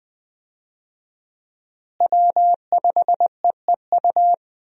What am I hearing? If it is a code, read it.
Morse code, heard as a keyed tone, about 20 wpm, 700 Hz. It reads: W5EEU
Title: USA Call Signs - 5 Characters - Repeated 2x - 20wpm